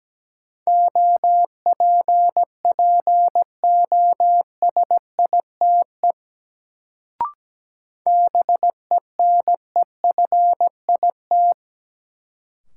Morse code, heard as a keyed tone, about 17 wpm, 700 Hz. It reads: OPPOSITE BENEFIT
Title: Sets of 5 Words Encore - New Word Drill - 17wpm